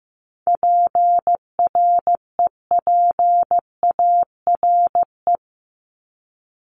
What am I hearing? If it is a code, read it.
Morse code, heard as a keyed tone, about 15 wpm, 700 Hz. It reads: PREPARE
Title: ED Suffix - Top 100 Words with Base Words 15wpm